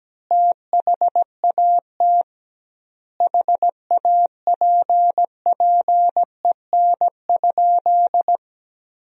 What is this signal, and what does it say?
Morse code, heard as a keyed tone, about 17 wpm, 700 Hz. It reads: THAT HAPPEN?
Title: 2x Word-Spacing - Sentences from Top 400 Words - 17wpm